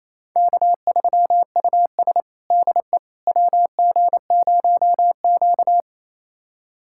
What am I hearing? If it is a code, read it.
Morse code, heard as a keyed tone, about 28 wpm, 700 Hz. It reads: K3US DE WG0Q
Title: USA Callsign DE Callsign - 4 Characters - 28wpm